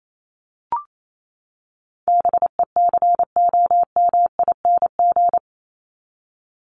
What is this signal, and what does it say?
Morse code, heard as a keyed tone, about 28 wpm, 700 Hz. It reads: BECOMING
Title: ING Suffix - Top 100 Words with Base Words 28wpm